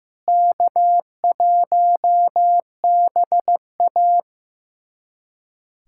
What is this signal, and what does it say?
Morse code, heard as a keyed tone, about 15 wpm, 700 Hz. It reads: K1BA
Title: USA Callsign DE Callsign - 4 Characters - 15wpm